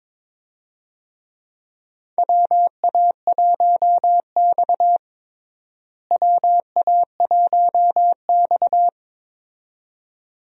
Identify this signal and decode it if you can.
Morse code, heard as a keyed tone, about 21 wpm, 700 Hz. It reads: WA1X WA1X
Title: USA Call Signs - 4 Characters - Repeated 2x - 22wpm